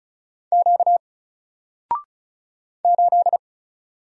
Morse code, heard as a keyed tone, about 35 wpm, 700 Hz. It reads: Q 8
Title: Single Character (Letter:Number:Punctuation:Prosign) 35wpm